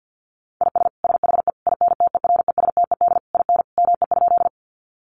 Morse code, heard as a keyed tone, about 48 wpm, 700 Hz. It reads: IS SHE INTERESTED IN ME?
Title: Sentences from Top 900 Words 50wpm